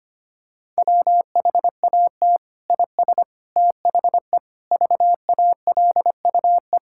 Morse code, heard as a keyed tone, about 25 wpm, 700 Hz. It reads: WHAT IS THE VALUE
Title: Sentences from Top 1000 Words - Review Part 6 - 25wpm